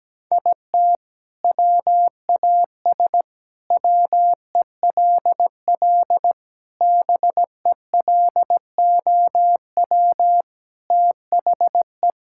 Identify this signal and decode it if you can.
Morse code, heard as a keyed tone, about 17 wpm, 700 Hz. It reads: IT WAS WELL BELOW THE
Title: Sentences from Top 1000 Words - Review Part 6 - 17wpm